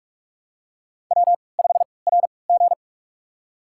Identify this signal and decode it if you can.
Morse code, heard as a keyed tone, about 46 wpm, spaced as for 25 wpm, 700 Hz. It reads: W5RG
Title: USA Call Signs - 4 Characters - 3x Character Spacing - 45wpm